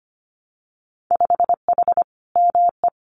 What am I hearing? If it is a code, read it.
Morse code, heard as a keyed tone, about 25 wpm, 700 Hz. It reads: 5H ME
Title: Winter Field Day Exchanges Encore 25wpm